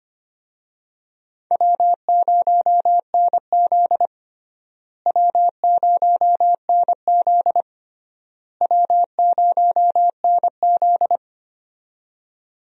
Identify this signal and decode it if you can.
Morse code, heard as a keyed tone, about 25 wpm, 700 Hz. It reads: W0NZ W0NZ W0NZ
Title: USA Call Signs - 4 Characters - Repeated 3x - 25wpm